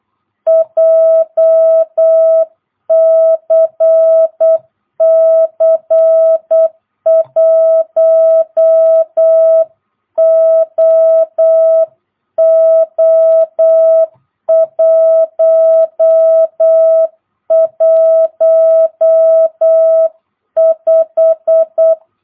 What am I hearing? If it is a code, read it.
Morse code, heard as a keyed tone, about 8 wpm, 660 Hz. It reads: JCC 1OO115
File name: jh8cbh / 2022cw